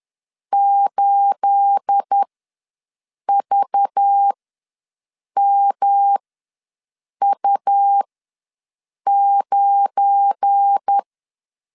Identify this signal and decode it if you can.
Morse code, heard as a keyed tone, about 10 wpm, spaced as for 6 wpm, 800 Hz. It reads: 8VMU9
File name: LEZIONI / 9 28 ws 10wpm 4124